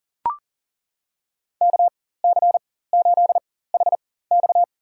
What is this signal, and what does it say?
Morse code, heard as a keyed tone, about 40 wpm, spaced as for 19 wpm, 700 Hz. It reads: KC8HX
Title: USA Call Signs - 4x Character Spacing - 40wpm